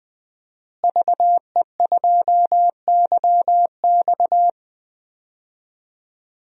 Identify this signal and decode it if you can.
Morse code, heard as a keyed tone, about 20 wpm, 700 Hz. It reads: VE2YX
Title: DXCC-100 Call Signs - 5 Characters - Repeated 2x - 20wpm